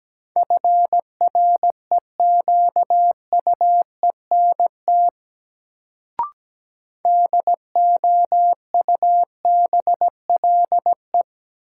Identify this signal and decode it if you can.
Morse code, heard as a keyed tone, about 17 wpm, 700 Hz. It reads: FREQUENT DOUBLE
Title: Sentences from Top 1400 Words - New Word Drill - 17wpm